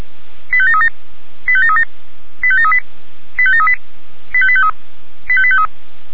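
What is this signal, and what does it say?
Five-tone selective call (ZVEI), heard as quick runs of five tones, tones of about 70 ms. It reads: 85737 85737 85738 85738 85853 85853
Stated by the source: ZVEI five-tone selective call received on an SDR.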